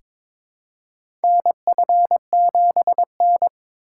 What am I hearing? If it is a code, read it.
Morse code, heard as a keyed tone, about 22 wpm, 700 Hz. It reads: NF7N
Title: USA Call Signs - 4 Characters - 2x Character Spacing - 22wpm